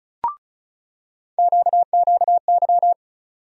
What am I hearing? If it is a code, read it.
Morse code, heard as a keyed tone, about 36 wpm, 700 Hz. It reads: QQY
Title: USA Call Sign Postfixes - 3 Letters 35wpm